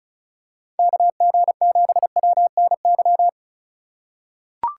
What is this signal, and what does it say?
Morse code, heard as a keyed tone, about 34 wpm, 700 Hz. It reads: KG7WNY
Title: USA Call Signs - 6 Characters - 2x Character Spacing - 35wpm